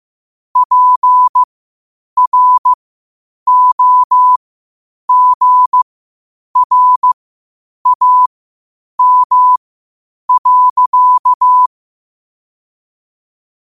Morse code, heard as a keyed tone, about 15 wpm, spaced as for 8 wpm, 1000 Hz. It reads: PROGRAM.
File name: qotdfiles / 8wpm